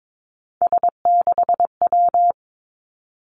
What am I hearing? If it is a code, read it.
Morse code, heard as a keyed tone, about 22 wpm, 700 Hz. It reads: S6W
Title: Three Challenging Letter-Number Combinations 22wpm